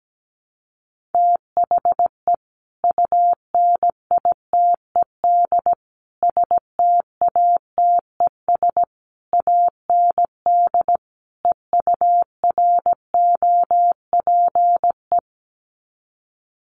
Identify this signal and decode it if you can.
Morse code, heard as a keyed tone, about 17 wpm, 700 Hz. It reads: THE UNITED STATES AND EUROPE
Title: Sets of 5 Words Encore 17wpm